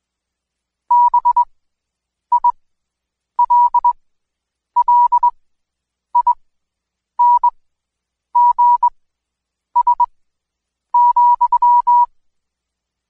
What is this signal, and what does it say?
Morse code, heard as a keyed tone, about 21 wpm, spaced as for 8 wpm, 970 Hz. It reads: BILLINGS,
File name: Code Quick / AudioDownld2 / Seg114-Day19-DayDrill 19